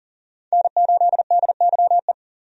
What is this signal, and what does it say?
Morse code, heard as a keyed tone, about 40 wpm, 700 Hz. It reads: N8DYE
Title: USA Call Signs - 5 Characters - 40wpm